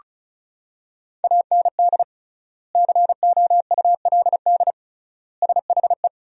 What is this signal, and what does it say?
Morse code, heard as a keyed tone, about 34 wpm, 700 Hz. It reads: AND COULD SHE
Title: 3x Word-Spacing - Sentences from Top 100 Words - 35wpm